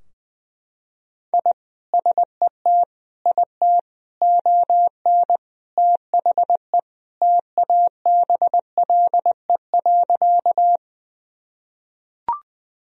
Morse code, heard as a keyed tone, about 20 wpm, 700 Hz. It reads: I SET IT ON THE TABLE.
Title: Sentences from Top 1000 Words - Review Part 2 - 20wpm